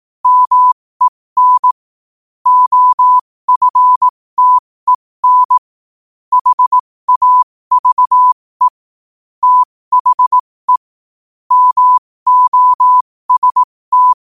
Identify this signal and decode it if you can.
Morse code, heard as a keyed tone, about 18 wpm, spaced as for 15 wpm, 1000 Hz. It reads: MEN OFTEN HAVE THE MOST
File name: qotdfiles / 15wpm